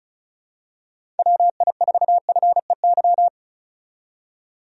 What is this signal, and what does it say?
Morse code, heard as a keyed tone, about 36 wpm, 700 Hz. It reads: WI4FEY
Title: USA Call Signs - 6 Characters - 35wpm